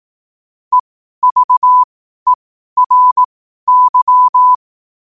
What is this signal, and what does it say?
Morse code, heard as a keyed tone, about 18 wpm, spaced as for 12 wpm, 1000 Hz. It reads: EVERY
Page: New quotes every day in morse code at 12 Words per minute.